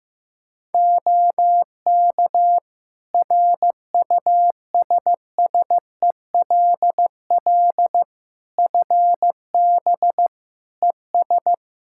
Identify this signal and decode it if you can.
Morse code, heard as a keyed tone, about 15 wpm, 700 Hz. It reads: OK RUSSELL FB ES
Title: Practice QSO - Exchange 2 - Info - 15wpm